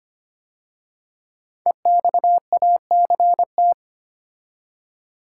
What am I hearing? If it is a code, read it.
Morse code, heard as a keyed tone, about 25 wpm, 700 Hz. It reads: EXACT
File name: 5 Letter Words - First 3 Letters Given - 25wpm